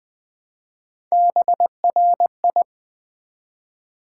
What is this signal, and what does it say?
Morse code, heard as a keyed tone, about 20 wpm, 700 Hz. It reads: BRI
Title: Build A Word - 1 Letter At A Time - 5-Letter Words - 20wpm